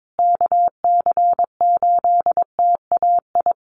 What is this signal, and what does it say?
Morse code, heard as a keyed tone, about 22 wpm, 700 Hz. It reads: KC8TAI
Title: USA Call Signs Repeated 3x 22wpm